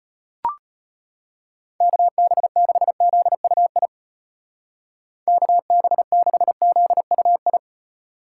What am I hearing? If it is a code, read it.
Morse code, heard as a keyed tone, about 32 wpm, 700 Hz. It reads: KB6ZUI KB6ZUI